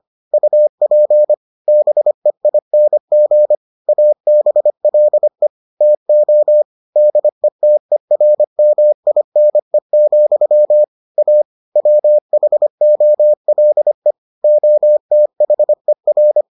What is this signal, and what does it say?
Morse code, heard as a keyed tone, about 25 wpm, 600 Hz. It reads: UP BEING ABLE TO DETERMINE, A WHOLE OTHER